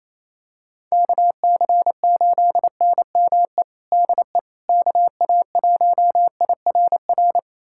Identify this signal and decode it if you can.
Morse code, heard as a keyed tone, about 28 wpm, 700 Hz. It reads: KC8NME DE KA1IRR
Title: USA Callsign DE Callsign - 6 Characters - 28wpm